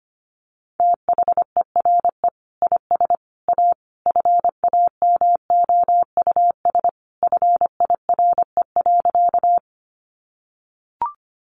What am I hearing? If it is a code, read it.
Morse code, heard as a keyed tone, about 25 wpm, 700 Hz. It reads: THERE IS A FAMOUS FIRE.
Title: Sentences from Top 1500 Words 25wpm